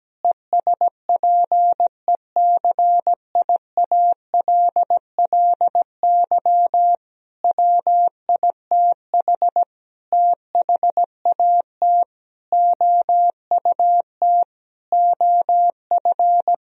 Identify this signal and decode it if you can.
Morse code, heard as a keyed tone, about 17 wpm, 700 Hz. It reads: ESPECIALLY WITH THAT OUT OF
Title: Sentences from Top 1000 Words - Review Part 4 - 17wpm